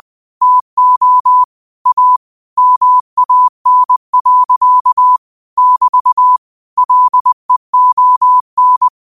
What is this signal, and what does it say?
Morse code, heard as a keyed tone, about 20 wpm, 1000 Hz. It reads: TO A MAN. <BT> LEON
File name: qotdfiles / 20wpm